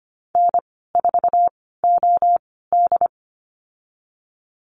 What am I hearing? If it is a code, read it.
Morse code, heard as a keyed tone, about 25 wpm, spaced as for 16 wpm, 700 Hz. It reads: N4OD
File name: Lesson 15g - Number 5 - Call Signs 25wpm